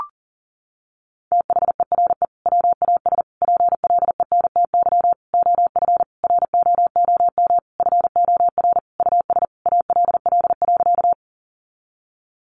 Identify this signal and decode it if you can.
Morse code, heard as a keyed tone, about 40 wpm, 700 Hz. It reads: THERE WAS PLENTY OF ROOM FOR US ALL.
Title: Sentences from Top 1300 Words 40wpm